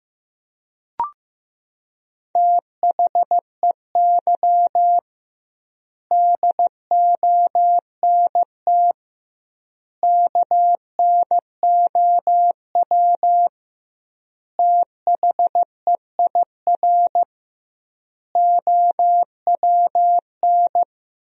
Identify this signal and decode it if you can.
Morse code, heard as a keyed tone, about 15 wpm, 700 Hz. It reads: THEY DONT KNOW THEIR OWN
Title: 2x Word-Spacing - Sentences from Top 300 Words - 15wpm